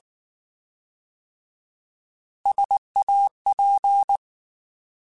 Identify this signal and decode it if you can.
Morse code, heard as a keyed tone, about 19 wpm, 800 Hz. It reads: SAP
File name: Alphabetical / Alphabetical mp3s / Ss @18